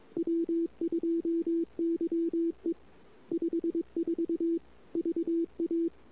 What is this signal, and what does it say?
Morse code, heard as a keyed tone, about 21 wpm, 340 Hz. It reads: W2YE 54 VA
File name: CW / 2008-ss-cw-k5tr-op-n5rz -23:00